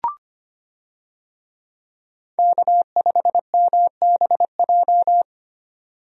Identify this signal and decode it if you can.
Morse code, heard as a keyed tone, about 25 wpm, 700 Hz. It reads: K5MBJ
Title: USA Call Signs - 5 Characters - 25wpm